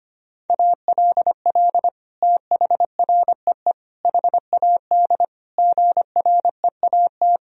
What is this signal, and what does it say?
Morse code, heard as a keyed tone, about 25 wpm, 700 Hz. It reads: ALL THREE HAD GREAT
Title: Sentences from Top 1000 Words - Review Part 2 - 25wpm